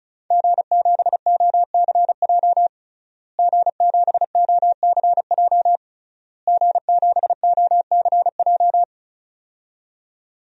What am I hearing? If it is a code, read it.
Morse code, heard as a keyed tone, about 36 wpm, 700 Hz. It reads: G7OCJ G7OCJ G7OCJ
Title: DXCC-100 Call Signs - 5 Characters - Repeated 3x - 35wpm